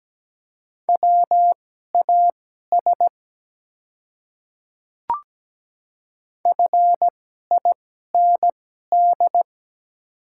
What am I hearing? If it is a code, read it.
Morse code, heard as a keyed tone, about 17 wpm, spaced as for 12 wpm, 700 Hz. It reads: WAS FIND
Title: 2x Character Spacing - Top 100 Words - 17wpm